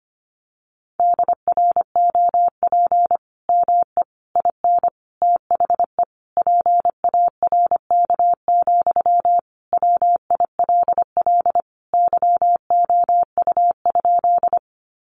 Morse code, heard as a keyed tone, about 25 wpm, 700 Hz. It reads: DROP ME IN THE PARK, WILL YOU?
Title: Sentences from Top 1400 Words 25wpm